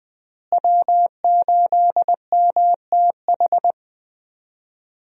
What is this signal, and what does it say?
Morse code, heard as a keyed tone, about 21 wpm, 700 Hz. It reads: W8MTH
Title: USA Call Signs - 5 Characters - Repeated 2x - 20wpm